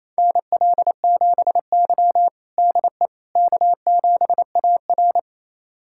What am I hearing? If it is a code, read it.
Morse code, heard as a keyed tone, about 28 wpm, 700 Hz. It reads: NL7Y DE K7AR
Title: USA Callsign DE Callsign - 4 Characters - 28wpm